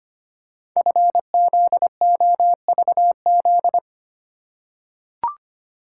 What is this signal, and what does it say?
Morse code, heard as a keyed tone, about 25 wpm, 700 Hz. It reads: FZOVZ
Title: Code Groups - 5 Letters - 25wpm